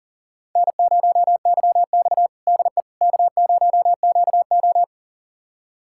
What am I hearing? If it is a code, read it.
Morse code, heard as a keyed tone, about 40 wpm, 700 Hz. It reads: N0YX DE K0QO
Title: USA Callsign DE Callsign - 4 Characters - 40wpm